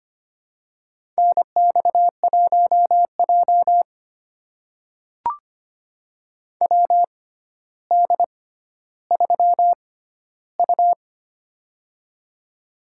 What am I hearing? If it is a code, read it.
Morse code, heard as a keyed tone, about 25 wpm, spaced as for 13 wpm, 700 Hz. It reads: NX1J WD3U
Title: USA Call Signs - 4 Characters - 6x Character Spacing - 25wpm